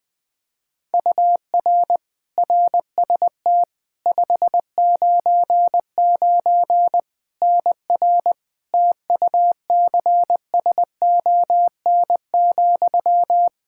Morse code, heard as a keyed tone, about 20 wpm, 700 Hz. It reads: UR RST 599 NR TUCSON,
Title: Practice QSO - Exchange 1 - Beginning - 20wpm